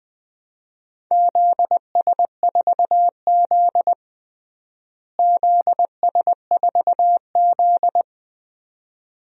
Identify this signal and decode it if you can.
Morse code, heard as a keyed tone, about 20 wpm, 700 Hz. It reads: ZS4Z ZS4Z
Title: DXCC-100 Call Signs - 4 Characters - Repeated 2x - 20wpm